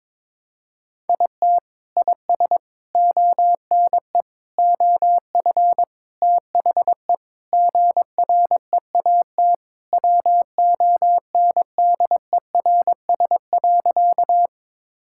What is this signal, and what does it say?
Morse code, heard as a keyed tone, about 22 wpm, 700 Hz. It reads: IT IS ONE OF THE GREAT WONDERS.
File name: practice / 22wpm / Sentences with 4-Word Phrases - 22wpm